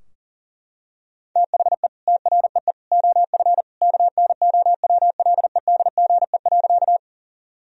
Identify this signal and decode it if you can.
Morse code, heard as a keyed tone, about 40 wpm, 700 Hz. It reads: THE TREE OF KNOWLEDGE.
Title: Sentences from Top 1000 Words - Review Part 3 - 40wpm